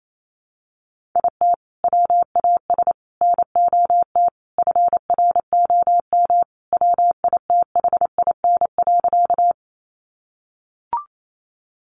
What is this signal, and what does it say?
Morse code, heard as a keyed tone, about 28 wpm, 700 Hz. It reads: IT WAS NOT FROM WITHIN.
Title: Sentences from Top 700 Words 28wpm